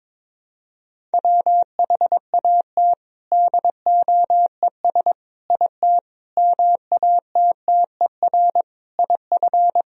Morse code, heard as a keyed tone, about 22 wpm, 700 Hz. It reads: WHAT DOES IT MATTER IF
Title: Sentences from Top 1000 Words - Review Part 1 - 22wpm